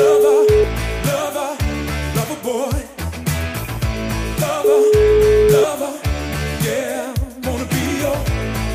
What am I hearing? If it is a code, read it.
Ringback tone (Europe, 425 Hz), with noise.